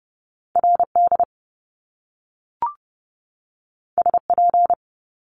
Morse code, heard as a keyed tone, about 30 wpm, 700 Hz. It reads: RD SP
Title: Top Two Letter Combinations in Words 30wpm